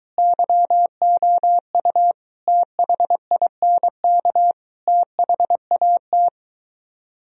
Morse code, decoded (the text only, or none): YOU THINK THAT